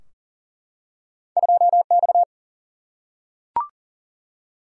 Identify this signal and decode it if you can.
Morse code, heard as a keyed tone, about 39 wpm, 700 Hz. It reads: 2 X
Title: Single-Letter Single-Number 40wpm